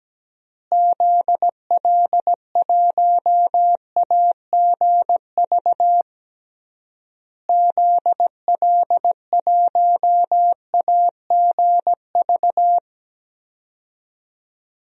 Morse code, heard as a keyed tone, about 17 wpm, 700 Hz. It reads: ZL1AGV ZL1AGV
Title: DXCC-100 Call Signs - 6 Characters - Repeated 3x - 17wpm